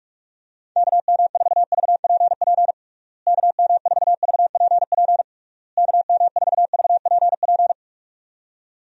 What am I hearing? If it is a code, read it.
Morse code, heard as a keyed tone, about 46 wpm, 700 Hz. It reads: KM4VPP KM4VPP KM4VPP
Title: USA Call Signs - 6 Characters - Repeated 3x - 45wpm